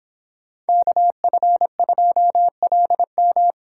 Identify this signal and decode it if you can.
Morse code, heard as a keyed tone, about 26 wpm, 700 Hz. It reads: KF2LM